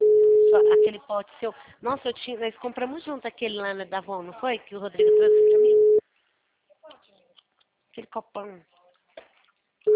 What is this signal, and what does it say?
Special information tone, with speech.